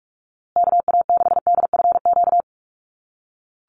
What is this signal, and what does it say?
Morse code, heard as a keyed tone, about 44 wpm, 700 Hz. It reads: KA6DFQ
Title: USA Call Signs Repeated 2x 45wpm